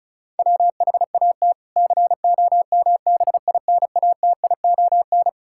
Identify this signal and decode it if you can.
Morse code, heard as a keyed tone, about 34 wpm, 700 Hz. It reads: WHAT COMBINATION
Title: Sentences from Top 1000 Words - Review Part 6 - 35wpm